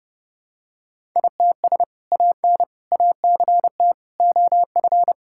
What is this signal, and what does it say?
Morse code, heard as a keyed tone, about 30 wpm, 700 Hz. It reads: ITS AN ACT OF
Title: Sentences from Top 1000 Words - Review Part 5 - 30wpm